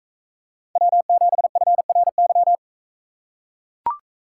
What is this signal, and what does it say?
Morse code, heard as a keyed tone, about 39 wpm, 700 Hz. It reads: W7FRY